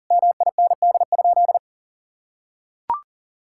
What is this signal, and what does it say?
Morse code, heard as a keyed tone, about 40 wpm, 700 Hz. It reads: MIND?
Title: Sentences from Top 1000 Words - Review Part 5 - 40wpm